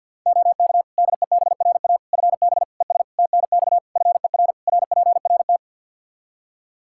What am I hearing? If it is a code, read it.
Morse code, heard as a keyed tone, about 48 wpm, 700 Hz. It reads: OK DEBRA FB ES TNX FER RPRT
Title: Practice QSO - Exchange 2 - Info - 50wpm